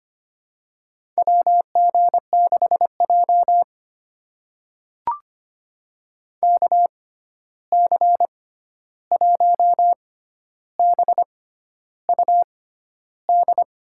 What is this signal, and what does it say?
Morse code, heard as a keyed tone, about 25 wpm, spaced as for 11 wpm, 700 Hz. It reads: WG6J KC1BUD
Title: USA Call Signs - 6x Character Spacing - 25wpm